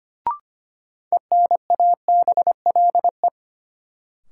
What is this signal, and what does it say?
Morse code, heard as a keyed tone, about 25 wpm, 700 Hz. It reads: ENABLE
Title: Sentences from Top 1500 Words - New Word Drill Alt - 25wpm